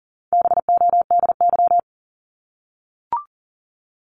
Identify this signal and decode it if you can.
Morse code, heard as a keyed tone, about 40 wpm, 700 Hz. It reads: BODY
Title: Top 500 Words Repeated 40wpm